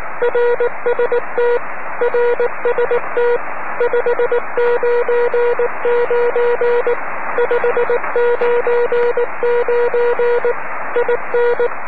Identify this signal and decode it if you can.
Morse code, heard as a keyed tone, about 19 wpm, 480 Hz, with noise. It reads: RST RST 599 599 IN
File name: om3pv 70mhz / zaznam / ea6vq